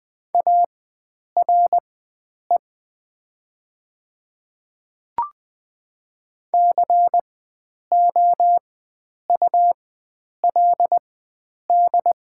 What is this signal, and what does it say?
Morse code, heard as a keyed tone, about 20 wpm, spaced as for 9 wpm, 700 Hz. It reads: ARE COULD
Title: 4x Character Spacing - Top 300 Words - 20wpm